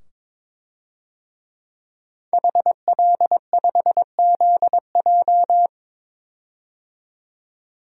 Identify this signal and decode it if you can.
Morse code, heard as a keyed tone, about 22 wpm, 700 Hz. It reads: HL5ZJ
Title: DXCC-100 Call Signs - 5 Characters - 22wpm